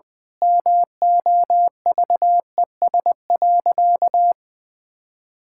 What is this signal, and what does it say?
Morse code, heard as a keyed tone, about 20 wpm, 700 Hz. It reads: MOVES.
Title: Pangram Challenge v2 - 20wpm